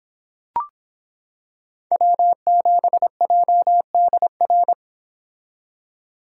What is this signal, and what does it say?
Morse code, heard as a keyed tone, about 25 wpm, 700 Hz. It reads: W7JDR